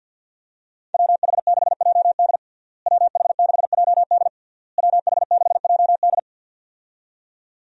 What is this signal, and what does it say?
Morse code, heard as a keyed tone, about 52 wpm, 700 Hz. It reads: WH6JD WH6JD WH6JD
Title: USA Call Signs - 5 Characters - Repeated 3x - 50wpm